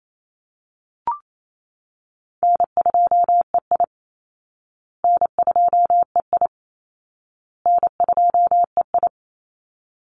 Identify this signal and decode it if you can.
Morse code, heard as a keyed tone, about 28 wpm, 700 Hz. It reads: N2EI N2EI N2EI
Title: Easy Call Signs Repeated 3x - 28wpm